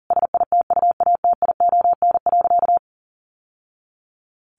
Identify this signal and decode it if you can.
Morse code, heard as a keyed tone, about 40 wpm, 700 Hz. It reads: SITUATION.
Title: Sentences from Top 1200 Words 40wpm